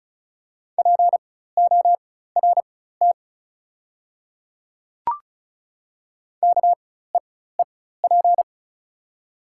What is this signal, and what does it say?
Morse code, heard as a keyed tone, about 36 wpm, spaced as for 16 wpm, 690 Hz. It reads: PORT KEEP
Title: 4x Character Spacing - Top 500 Words - 35wpm